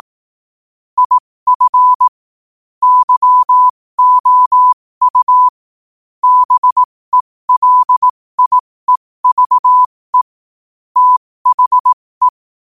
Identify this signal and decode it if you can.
Morse code, heard as a keyed tone, about 18 wpm, spaced as for 15 wpm, 1000 Hz. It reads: IF YOU BELIEVE THE
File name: qotdfiles / 15wpm